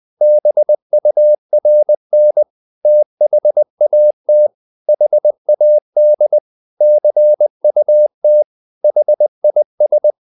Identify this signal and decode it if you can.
Morse code, heard as a keyed tone, about 20 wpm, 600 Hz. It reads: BURN THAT HAD CUT HIS
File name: the old man and the sea / the old man and the sea part 1 20wpm